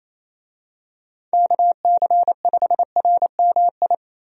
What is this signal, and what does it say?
Morse code, heard as a keyed tone, about 28 wpm, 700 Hz. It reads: KC5RMI
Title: USA Call Signs - 6 Characters - Repeated 3x - 28wpm